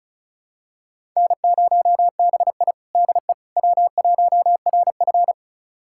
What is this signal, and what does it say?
Morse code, heard as a keyed tone, about 36 wpm, 700 Hz. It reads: N0BI DE W1RF
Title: USA Callsign DE Callsign - 4 Characters - 35wpm